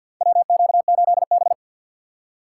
Morse code, heard as a keyed tone, about 52 wpm, 700 Hz. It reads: WQ8B
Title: USA Call Signs - 4 Characters - 3x Character Spacing - 50wpm